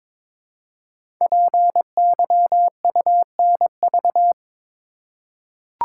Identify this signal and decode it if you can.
Morse code, heard as a keyed tone, about 22 wpm, 700 Hz. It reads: PYUNV
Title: Code Groups - 5 Letters - 22wpm